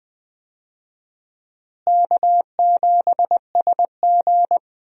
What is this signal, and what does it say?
Morse code, heard as a keyed tone, about 20 wpm, 700 Hz. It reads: K7SG
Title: USA Call Signs - 4 Characters - Repeated 3x - 20wpm